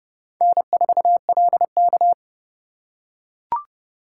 Morse code, heard as a keyed tone, about 29 wpm, 700 Hz. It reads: N4LK